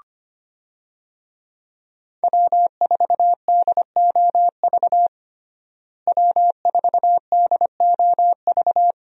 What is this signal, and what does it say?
Morse code, heard as a keyed tone, about 25 wpm, 700 Hz. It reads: W4DOV W4DOV
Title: USA Call Signs - 5 Characters - Repeated 3x - 25wpm